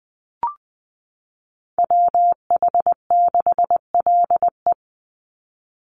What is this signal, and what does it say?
Morse code, heard as a keyed tone, about 20 wpm, 700 Hz. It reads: WH6LE
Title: NA SOTA Chaser Callsigns 20wpm